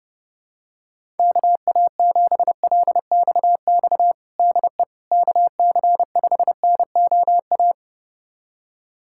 Morse code, heard as a keyed tone, about 32 wpm, 700 Hz. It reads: KA7LXX DE KC5NOA
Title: USA Callsign DE Callsign 30wpm